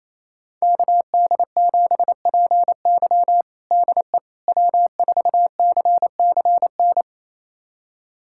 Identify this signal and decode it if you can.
Morse code, heard as a keyed tone, about 27 wpm, 700 Hz. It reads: KD7PY DE W4CCN
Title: USA Callsign DE Callsign - 5 Characters - 28wpm